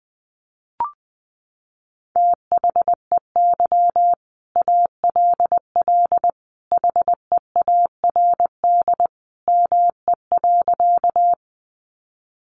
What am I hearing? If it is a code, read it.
Morse code, heard as a keyed tone, about 19 wpm, 700 Hz. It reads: THEY ALL HEARD ME.
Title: Sentences from Top 400 Words 20wpm